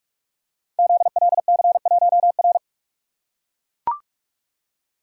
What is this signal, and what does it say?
Morse code, heard as a keyed tone, about 44 wpm, 690 Hz. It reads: ZLC1R
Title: Code Groups - 5 Letter:Numbers - 45wpm